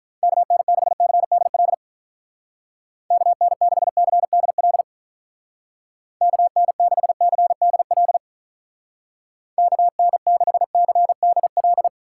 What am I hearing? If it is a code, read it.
Morse code, heard as a keyed tone, about 44 wpm, 700 Hz. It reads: KN6CDL KN6CDL KN6CDL KN6CDL
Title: Speed Racing - USA 2x3 Call Signs 35wpm